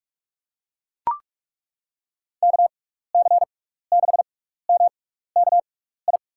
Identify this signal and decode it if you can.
Morse code, heard as a keyed tone, about 42 wpm, spaced as for 15 wpm, 700 Hz. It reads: KC6MKI
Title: USA Call Signs - 6x Character Spacing - 45wpm